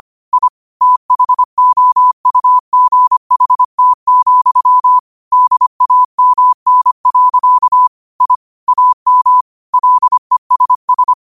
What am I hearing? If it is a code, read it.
Morse code, heard as a keyed tone, about 25 wpm, 1000 Hz. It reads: I THOUGHT, DAMN. I AM LESS
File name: qotdfiles / 25wpm